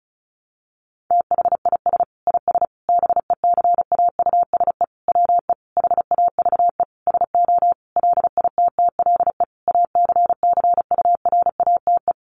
Morse code, heard as a keyed tone, about 34 wpm, 700 Hz. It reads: THIS IS BECAUSE WE HAVE SO LITTLE ACCURATE
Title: Sentences from Top 1200 Words 35wpm